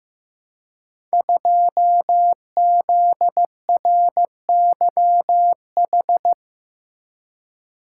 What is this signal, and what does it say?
Morse code, heard as a keyed tone, about 15 wpm, 700 Hz. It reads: 2ZRYH
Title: Code Groups - 5 Letter:Numbers - 15wpm